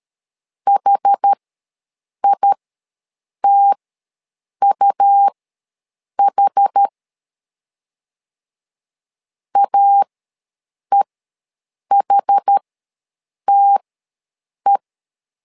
Morse code, heard as a keyed tone, about 13 wpm, spaced as for 7 wpm, 800 Hz. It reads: HITUH AEHTE
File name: LEZIONI / 2 UHDN / 2 UH ws 12wpm 1730